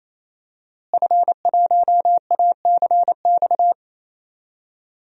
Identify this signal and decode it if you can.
Morse code, heard as a keyed tone, about 28 wpm, 700 Hz. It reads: F1ACX
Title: DXCC-100 Call Signs - 5 Characters - Repeated 3x - 28wpm